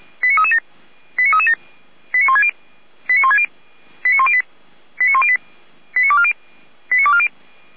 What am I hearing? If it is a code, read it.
Five-tone selective call (ZVEI), heard as quick runs of five tones, tones of about 80 ms. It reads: 89337 89337 89160 89160 89108 89108 89240 89240